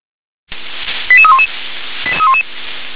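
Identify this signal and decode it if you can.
Five-tone selective call (ZVEI), heard as quick runs of five tones, tones of about 70 ms. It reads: 88311
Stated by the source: ZVEI five-tone selective call received on an SDR.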